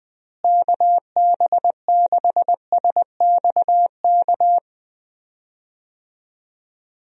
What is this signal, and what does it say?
Morse code, heard as a keyed tone, about 20 wpm, 700 Hz. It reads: KB6SXK
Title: USA Call Signs - 6 Characters - Repeated 2x - 20wpm